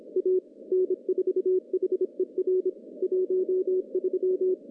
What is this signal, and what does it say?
Morse code, heard as a keyed tone, about 27 wpm, 370 Hz. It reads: A N4HER 13
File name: contests / CW-SS-k5tr- -0100